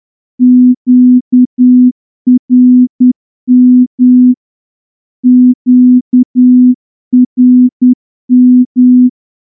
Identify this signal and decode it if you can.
Morse code, heard as a keyed tone, about 10 wpm, 250 Hz. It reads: QRM QRM
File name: morse-training-10WPM